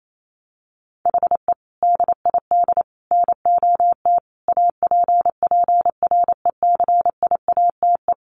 Morse code, heard as a keyed tone, about 28 wpm, 700 Hz. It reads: HE DID NOT APPRECIATE
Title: Sentences from Top 1300 Words 28wpm